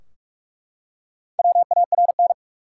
Morse code, heard as a keyed tone, about 42 wpm, 700 Hz. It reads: WARN
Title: Four Letter Words - 45wpm